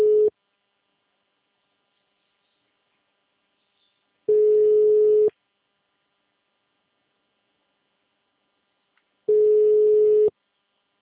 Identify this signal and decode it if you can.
Ringback tone (Europe, 425 Hz).
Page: Santa Gertrudes - SP